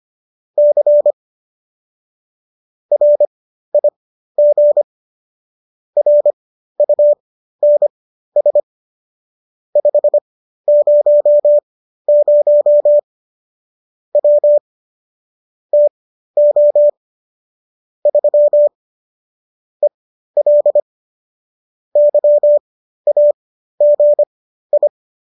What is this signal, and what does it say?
Morse code, heard as a keyed tone, about 25 wpm, spaced as for 13 wpm, 590 Hz. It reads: C RIG RUNS 500 W TO 3 EL YAGI